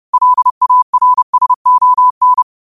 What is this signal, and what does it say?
Morse code, heard as a keyed tone, about 30 wpm, 1000 Hz. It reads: LARSON
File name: qotdfiles / 30wpm